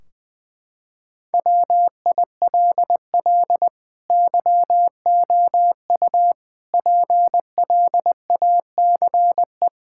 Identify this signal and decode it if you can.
Morse code, heard as a keyed tone, about 20 wpm, 700 Hz. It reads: WILL YOU PLACE
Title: Sentences from Top 1600 Words 20wpm